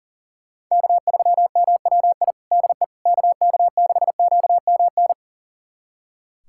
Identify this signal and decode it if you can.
Morse code, heard as a keyed tone, about 40 wpm, 700 Hz. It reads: K3MWI DE KK6QMN
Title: USA Callsign DE Callsign 40wpm